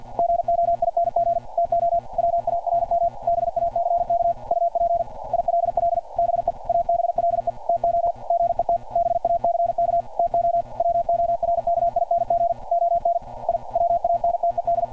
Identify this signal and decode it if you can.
Morse code, heard as a keyed tone, about 50 wpm, 690 Hz. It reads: GOING TO WARM BACK UP ILL BE ABLE TO GET BACK TO WORKING ON I WANTED
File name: archived / CRSnet